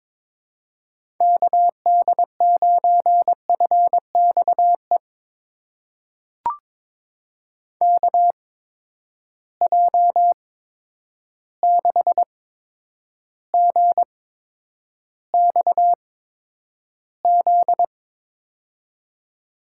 Morse code, heard as a keyed tone, about 21 wpm, spaced as for 9 wpm, 700 Hz. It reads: KD9FXE KJ6GXZ
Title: USA Call Signs - 6 Characters - 8x Character Spacing - 22wpm